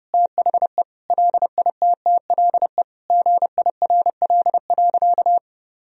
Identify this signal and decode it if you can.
Morse code, heard as a keyed tone, about 30 wpm, 700 Hz. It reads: THE LITTLE GIRL.
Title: Sentences from Top 1000 Words - Review Part 1 - 30wpm